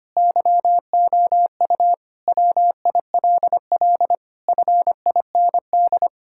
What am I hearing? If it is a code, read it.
Morse code, heard as a keyed tone, about 25 wpm, 700 Hz. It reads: YOU WILL FIND
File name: practice / 25wpm / Sentences with 3-Word Phrases - 25wpm